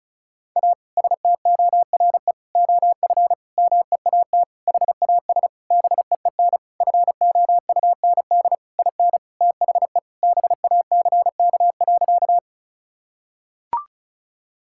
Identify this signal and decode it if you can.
Morse code, heard as a keyed tone, about 36 wpm, 700 Hz. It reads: A STORE OF MEAT HAS BEEN FOUND IN THE BACK.
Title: Sentences from Top 1000 Words - Review Part 5 - 35wpm